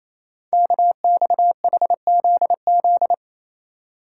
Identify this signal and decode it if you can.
Morse code, heard as a keyed tone, about 27 wpm, 700 Hz. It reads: KXHZZ
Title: Code Groups - 5 Letters - 28wpm